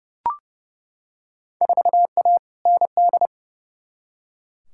Morse code, heard as a keyed tone, about 31 wpm, 700 Hz. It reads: 4A ND
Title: Field Day Exchanges Encore - 30wpm